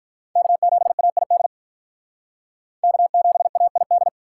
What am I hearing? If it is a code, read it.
Morse code, heard as a keyed tone, about 43 wpm, 700 Hz. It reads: K7AID K7AID
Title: Speed Racing - USA 1x3 Call Signs 35wpm